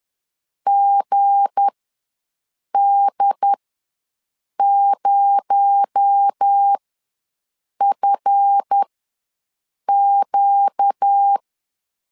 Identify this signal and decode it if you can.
Morse code, heard as a keyed tone, about 10 wpm, spaced as for 6 wpm, 800 Hz. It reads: GD0FQ
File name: LEZIONI / 9 2867 ws 10wpm 4215